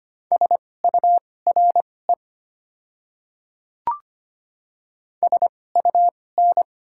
Morse code, heard as a keyed tone, about 25 wpm, spaced as for 18 wpm, 690 Hz. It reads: SURE SUN
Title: 2x Character Spacing - Top 300 Words - 25wpm